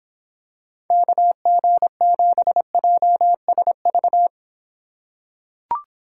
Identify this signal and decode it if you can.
Morse code, heard as a keyed tone, about 25 wpm, 700 Hz. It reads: KG7JSV